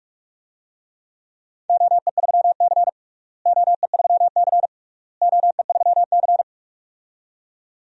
Morse code, heard as a keyed tone, about 44 wpm, 700 Hz. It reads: OE3C OE3C OE3C
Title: DXCC-100 Call Signs - 4 Characters - Repeated 3x - 45wpm